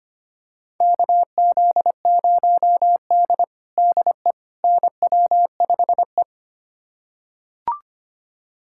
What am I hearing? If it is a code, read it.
Morse code, heard as a keyed tone, about 25 wpm, 700 Hz. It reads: KZ0D DE NW5E
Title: USA Callsign DE Callsign - 4 Characters - 25wpm